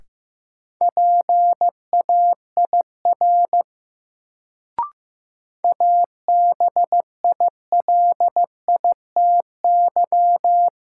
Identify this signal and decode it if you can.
Morse code, heard as a keyed tone, about 15 wpm, 700 Hz. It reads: PAIR ABILITY
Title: Sets of 3 Words - New Word Drill - 15wpm